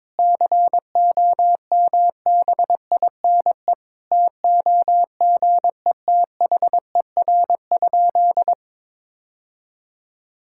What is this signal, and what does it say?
Morse code, heard as a keyed tone, about 22 wpm, 700 Hz. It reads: COMBINE TOGETHER?
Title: Sentences from Top 1600 Words 22wpm